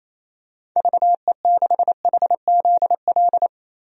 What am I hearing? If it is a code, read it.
Morse code, heard as a keyed tone, about 27 wpm, 700 Hz. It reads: VE6HZL
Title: DXCC-100 Call Signs - 6 Characters - 28wpm